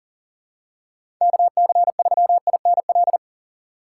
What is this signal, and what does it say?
Morse code, heard as a keyed tone, about 38 wpm, 700 Hz. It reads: KC3INL
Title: USA Call Signs - 6 Characters - 6x Character Spacing - 40wpm